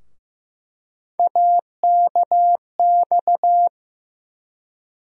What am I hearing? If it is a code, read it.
Morse code, heard as a keyed tone, about 15 wpm, 700 Hz. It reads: AKX
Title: USA Call Sign Postfixes - 3 Letters 15wpm